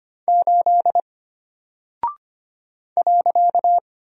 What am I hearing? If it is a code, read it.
Morse code, heard as a keyed tone, about 25 wpm, 700 Hz. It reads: 8 .
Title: Single Character (Letter:Number:Punctuation:Prosign) Alt 25wpm